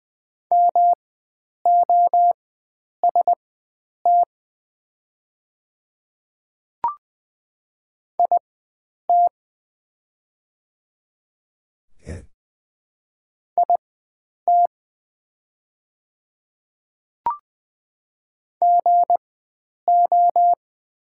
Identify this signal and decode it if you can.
Morse code, heard as a keyed tone, about 20 wpm, spaced as for 9 wpm, 700 Hz. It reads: MOST IT IT GO
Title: 4x Character Spacing - Top 100 Words - 20wpm